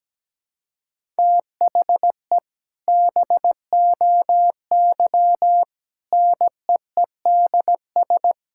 Morse code, heard as a keyed tone, about 17 wpm, 700 Hz. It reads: THE BOY NEEDS
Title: Sentences from Top 1600 Words 17wpm